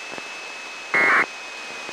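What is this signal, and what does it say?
Packet radio (AX.25, as in APRS).
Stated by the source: Transmission from the RAFT satellite during its re-entry campaign, received at the UC Irvine Ground Station, with Doppler shift.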